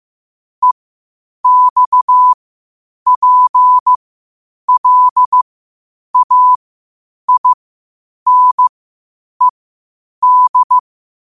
Morse code, heard as a keyed tone, about 15 wpm, spaced as for 8 wpm, 1000 Hz. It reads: EXPLAINED
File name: 8wpm